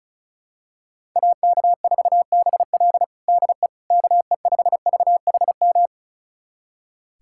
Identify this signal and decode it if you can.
Morse code, heard as a keyed tone, about 36 wpm, 700 Hz. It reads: AK4BL DE KE5VHM
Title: USA Callsign DE Callsign 35wpm